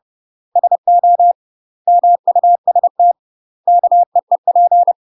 Morse code, heard as a keyed tone, about 30 wpm, 700 Hz. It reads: SO MUST KEEP
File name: UK MASTER QSOs TRAINING / 30WPM / UK MASTER QSOs 0019